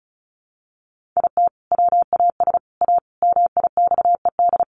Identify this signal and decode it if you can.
Morse code, heard as a keyed tone, about 34 wpm, 700 Hz. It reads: IT WAS A MIXED
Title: Sentences from Top 1500 Words 35wpm